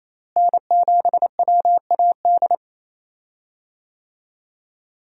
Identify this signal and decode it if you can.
Morse code, heard as a keyed tone, about 28 wpm, 700 Hz. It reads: N7WAD
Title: USA Call Signs - 5 Characters - Repeated 2x - 28wpm